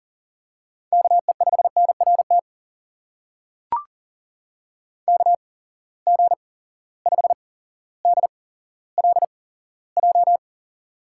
Morse code, heard as a keyed tone, about 41 wpm, spaced as for 17 wpm, 700 Hz. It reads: KE5NRT KG5DLJ
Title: USA Call Signs - 6 Characters - 8x Character Spacing - 40wpm